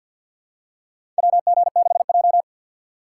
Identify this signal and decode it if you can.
Morse code, heard as a keyed tone, about 53 wpm, 700 Hz. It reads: WG6J
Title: USA Call Signs - 4 Characters - 6x Character Spacing - 50wpm